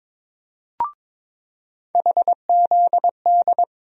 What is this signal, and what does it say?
Morse code, heard as a keyed tone, about 22 wpm, 700 Hz. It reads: HZD
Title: USA Call Sign Postfixes - 3 Letters 22wpm